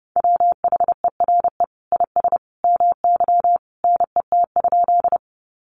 Morse code, heard as a keyed tone, about 28 wpm, 700 Hz. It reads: WHERE IS MY NET?
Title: Sentences from Top 1400 Words 30wpm